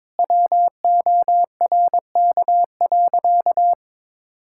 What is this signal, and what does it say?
Morse code, heard as a keyed tone, about 22 wpm, 700 Hz. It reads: WORK.
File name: practice / 22wpm / Sentences with 8-Word Phrases - 22wpm